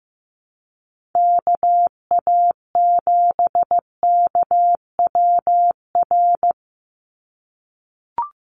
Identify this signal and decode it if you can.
Morse code, heard as a keyed tone, about 15 wpm, 700 Hz. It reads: KA7KWR
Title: USA Call Signs 15wpm